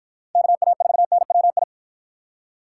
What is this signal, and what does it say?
Morse code, heard as a keyed tone, about 53 wpm, 700 Hz. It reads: KA4NWI